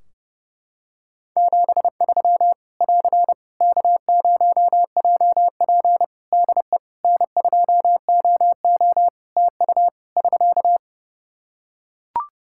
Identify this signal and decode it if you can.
Morse code, heard as a keyed tone, about 30 wpm, 700 Hz. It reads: 73 <AR> K0JP DE N2OO TU <SK>
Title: Practice QSO - Exchange 4 - Ending - 30wpm